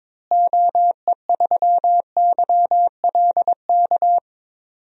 Morse code, heard as a keyed tone, about 22 wpm, 700 Hz. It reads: OE3YLK
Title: DXCC-100 Call Signs - 6 Characters - Repeated 3x - 22wpm